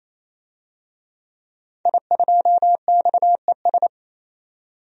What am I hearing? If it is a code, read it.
Morse code, heard as a keyed tone, about 28 wpm, 700 Hz. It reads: I2XES
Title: DXCC-100 Call Signs - 5 Characters - 28wpm